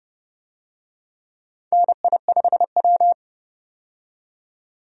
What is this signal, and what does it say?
Morse code, heard as a keyed tone, about 29 wpm, 700 Hz. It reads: NI5W
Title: USA Call Signs - 4 Characters - 30wpm